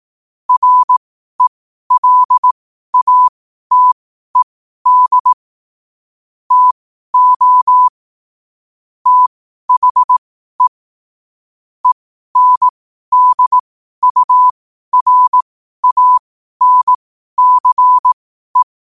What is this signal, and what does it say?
Morse code, heard as a keyed tone, about 18 wpm, spaced as for 12 wpm, 1000 Hz. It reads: RELATED TO THE ENDURANCE
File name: qotdfiles / 12wpm